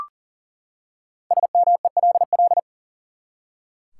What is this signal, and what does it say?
Morse code, heard as a keyed tone, about 40 wpm, 700 Hz. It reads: SMELL
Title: Five Letter Words - 40wpm